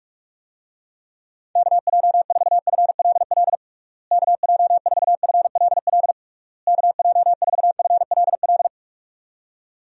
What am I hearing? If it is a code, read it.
Morse code, heard as a keyed tone, about 42 wpm, 700 Hz. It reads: KJ4FLL KJ4FLL KJ4FLL
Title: USA Call Signs - 6 Characters - Repeated 3x - 45wpm